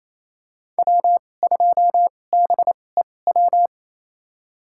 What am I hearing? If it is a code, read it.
Morse code, heard as a keyed tone, about 28 wpm, spaced as for 20 wpm, 700 Hz. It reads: W2BEW
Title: USA Call Signs - 5 Characters - 2x Character Spacing - 28wpm